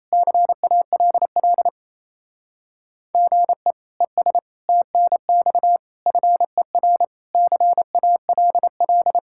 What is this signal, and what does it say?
Morse code, heard as a keyed tone, about 34 wpm, 700 Hz. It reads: CALL GE ES TNX FER CALL
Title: Speed-Racing Common QSO Phrases - 28wpm